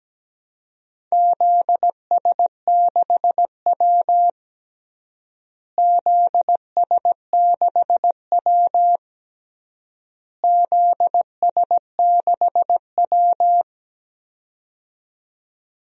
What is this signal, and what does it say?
Morse code, heard as a keyed tone, about 17 wpm, 700 Hz. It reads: ZS6W ZS6W ZS6W
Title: DXCC-100 Call Signs - 4 Characters - Repeated 3x - 17wpm